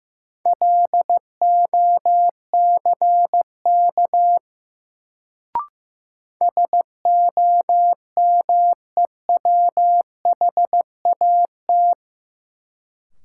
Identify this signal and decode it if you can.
Morse code, heard as a keyed tone, about 15 wpm, 700 Hz. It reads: LOCK SOMEWHAT
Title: Sentences from Top 1300 Words - New Word Drill Alt - 15wpm